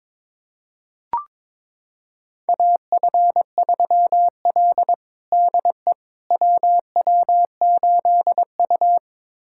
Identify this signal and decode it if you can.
Morse code, heard as a keyed tone, about 22 wpm, 700 Hz. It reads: AF3L DE WW8U
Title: USA Callsign DE Callsign - 4 Characters - 22wpm